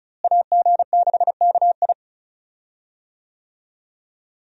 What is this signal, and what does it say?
Morse code, heard as a keyed tone, about 34 wpm, 700 Hz. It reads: AG6KI
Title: USA Call Signs - 5 Characters - Repeated 3x - 35wpm